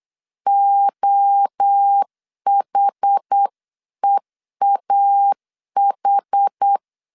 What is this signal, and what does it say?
Morse code, heard as a keyed tone, about 8 wpm, 800 Hz. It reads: OHEAH